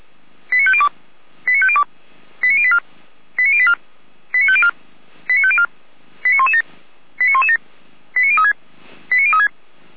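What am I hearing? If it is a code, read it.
Five-tone selective call (ZVEI), heard as quick runs of five tones, tones of about 70 ms. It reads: 89502 89502 89084 89084 89504 89504 89117 89117 89036 89036
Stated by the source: ZVEI five-tone selective call received on SDR.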